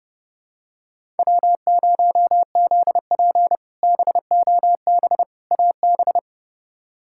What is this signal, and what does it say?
Morse code, heard as a keyed tone, about 30 wpm, 700 Hz. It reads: W0ZP BOB AB
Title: K1USN SST QSO Practice - 30wpm